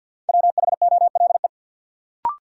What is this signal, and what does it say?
Morse code, heard as a keyed tone, about 53 wpm, 700 Hz. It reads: WHOLE
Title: Sentences from Top 400 Words - New Word Drill - 50wpm